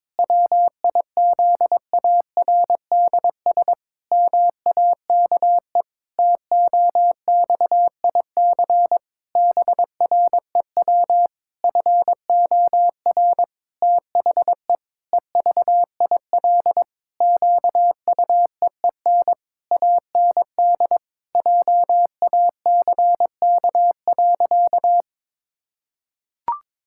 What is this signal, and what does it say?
Morse code, heard as a keyed tone, about 22 wpm, 700 Hz. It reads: WIZARDS MAKE TOXIC BREW FOR THE EVIL QUEEN AND JACK.
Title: Pangram Challenge v2 - 22wpm